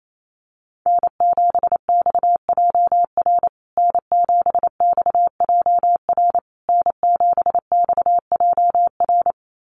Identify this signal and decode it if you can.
Morse code, heard as a keyed tone, about 28 wpm, 690 Hz. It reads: N7XJR N7XJR N7XJR
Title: USA Call Signs Repeated 3x 28wpm